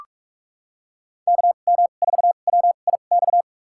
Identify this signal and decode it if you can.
Morse code, heard as a keyed tone, about 42 wpm, spaced as for 31 wpm, 700 Hz. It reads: KM4WIX
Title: USA Call Signs - 6 Characters - 2x Character Spacing - 45wpm